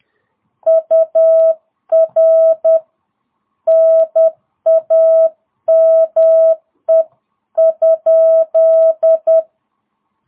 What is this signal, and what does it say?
Morse code, heard as a keyed tone, about 10 wpm, 660 Hz. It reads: UR NAME?